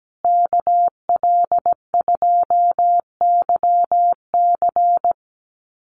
Morse code, heard as a keyed tone, about 17 wpm, 700 Hz. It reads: KL2YC
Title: USA Call Signs Repeated 3x 17wpm